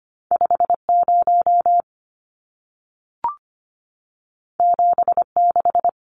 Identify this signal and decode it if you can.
Morse code, heard as a keyed tone, about 25 wpm, 700 Hz. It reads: 50 76
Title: Numbers - 2-Digit 25wpm